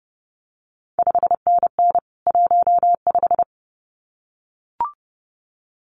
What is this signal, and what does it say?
Morse code, heard as a keyed tone, about 30 wpm, 700 Hz. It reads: 5NN 15
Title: CQ WW DX Contest Exchange Encore - 30wpm